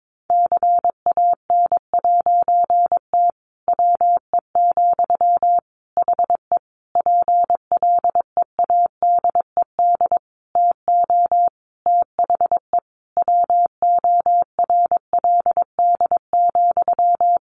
Morse code, heard as a keyed tone, about 22 wpm, 700 Hz. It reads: CAN'T WE, HE PLEADED TO THE WORLD,
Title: A Sound Of Thunder 22wpm